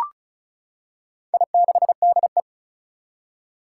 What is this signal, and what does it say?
Morse code, heard as a keyed tone, about 36 wpm, 700 Hz. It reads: I6DE
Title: Four Challenging Letter-Number Combinations 35wpm